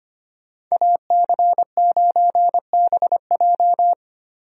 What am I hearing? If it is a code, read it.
Morse code, heard as a keyed tone, about 25 wpm, 700 Hz. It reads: AC9BJ
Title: USA Call Signs - 5 Characters - 25wpm